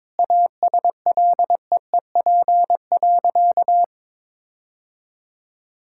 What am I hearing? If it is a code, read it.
Morse code, heard as a keyed tone, about 22 wpm, 700 Hz. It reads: ASLEEP.
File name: practice / 22wpm / Sentences with 3-Word Phrases - 22wpm